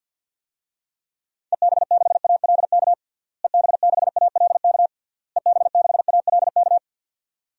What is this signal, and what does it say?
Morse code, heard as a keyed tone, about 48 wpm, 700 Hz. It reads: EB6ALK EB6ALK EB6ALK
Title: DXCC-100 Call Signs - 6 Characters - Repeated 3x - 50wpm